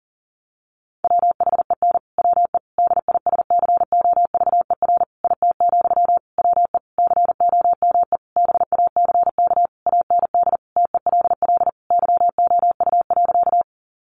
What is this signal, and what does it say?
Morse code, heard as a keyed tone, about 37 wpm, 700 Hz. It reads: WHEN WE DISCOVER IT, WE COME BACK AND TELL YOU.
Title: Sentences from Top 1100 Words 40wpm